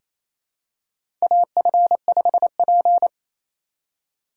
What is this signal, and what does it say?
Morse code, heard as a keyed tone, about 28 wpm, 700 Hz. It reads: AF5P
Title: K1USN SST QSO Practice - 28wpm